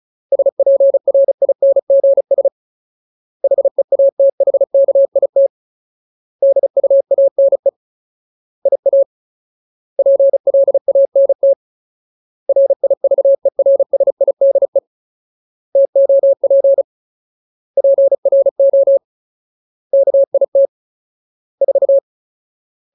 Morse code, heard as a keyed tone, about 36 wpm, 550 Hz. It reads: SPRINGS HEATHKIT DUANE IA PLANT RIVERSIDE TOP PRO KIT 4
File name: Ultimate Word List / 35WPM Ultimate